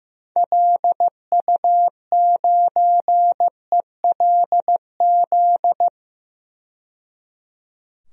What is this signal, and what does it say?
Morse code, heard as a keyed tone, about 15 wpm, 700 Hz. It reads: LU9ELZ
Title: DXCC-100 Call Signs - 6 Characters - 15wpm